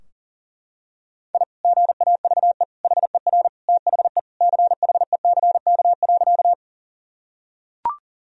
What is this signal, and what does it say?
Morse code, heard as a keyed tone, about 40 wpm, 700 Hz. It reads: I GAVE HER THE CHECK.
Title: Sentences from Top 1000 Words - Review Part 3 - 40wpm